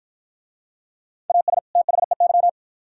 Morse code, heard as a keyed tone, about 47 wpm, 700 Hz. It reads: AS THEY